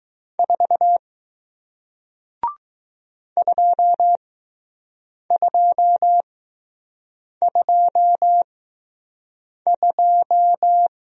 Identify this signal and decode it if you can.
Morse code, heard as a keyed tone, about 20 wpm, spaced as for 6 wpm, 700 Hz. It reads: 4 2222